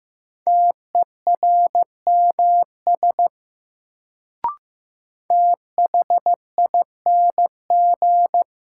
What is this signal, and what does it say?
Morse code, heard as a keyed tone, about 15 wpm, 700 Hz. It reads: TERMS THING
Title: Sets of 3 Words - New Word Drill - 15wpm